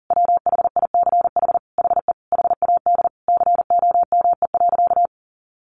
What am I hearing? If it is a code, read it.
Morse code, heard as a keyed tone, about 40 wpm, 700 Hz. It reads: WHICH HE HAD COME.
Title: Sentences from Top 1500 Words 40wpm